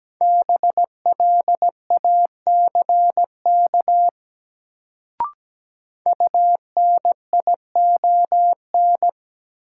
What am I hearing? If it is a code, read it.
Morse code, heard as a keyed tone, about 17 wpm, 700 Hz. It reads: BLACK UNION
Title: Sets of 5 Words Encore - New Word Drill - 17wpm